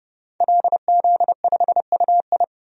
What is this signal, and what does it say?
Morse code, heard as a keyed tone, about 32 wpm, 700 Hz. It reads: LZ5UI
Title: DXCC-100 Call Signs - 5 Characters - 30wpm